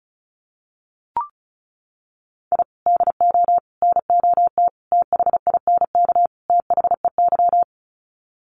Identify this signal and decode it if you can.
Morse code, heard as a keyed tone, about 34 wpm, 700 Hz. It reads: I DO NOT THINK THEY
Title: Sets of 5 Words 35wpm